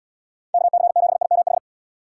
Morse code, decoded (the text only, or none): KC7IMH